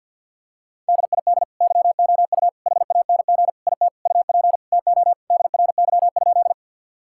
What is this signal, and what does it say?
Morse code, heard as a keyed tone, about 50 wpm, 700 Hz. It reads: DID YOU HANG IT UP TO DRY?
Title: Sentences from Top 1600 Words 50wpm